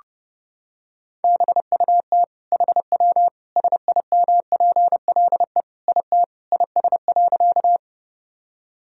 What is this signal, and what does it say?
Morse code, heard as a keyed tone, about 32 wpm, 700 Hz. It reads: BUT HW SIMPLE IT IS.
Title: Sentences from Top 400 Words and Abbreviations 30wpm